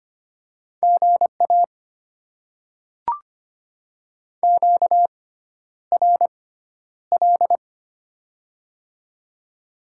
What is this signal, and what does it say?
Morse code, heard as a keyed tone, about 25 wpm, spaced as for 11 wpm, 700 Hz. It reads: GA QRL
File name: Top 20 QSO Elements - 6x Character Spacing - 25wpm